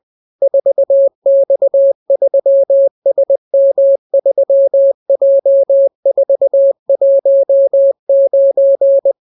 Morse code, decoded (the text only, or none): 4X3SM3J419